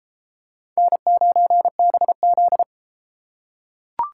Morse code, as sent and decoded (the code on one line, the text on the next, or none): -. ----. -... --..
N9BZ